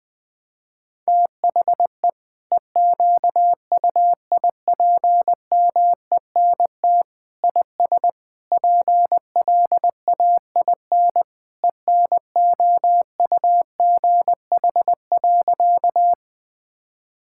Morse code, heard as a keyed tone, about 20 wpm, 700 Hz. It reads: THE EQUIPMENT IS PLAIN ENOUGH.
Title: Sentences from Top 1000 Words - Review Part 5 - 20wpm